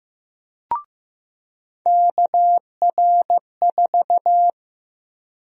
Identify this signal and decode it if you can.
Morse code, heard as a keyed tone, about 15 wpm, 700 Hz. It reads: KR4
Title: USA Call Sign Prefixes - 15wpm